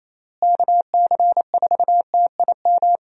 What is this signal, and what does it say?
Morse code, heard as a keyed tone, about 28 wpm, 700 Hz. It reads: KC4TIM
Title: USA Call Signs - 6 Characters - Repeated 3x - 28wpm